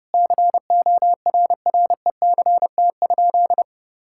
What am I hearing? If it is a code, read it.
Morse code, heard as a keyed tone, about 30 wpm, 700 Hz. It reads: CORRECT?
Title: Sentences from Top 1000 Words - Review Part 5 - 30wpm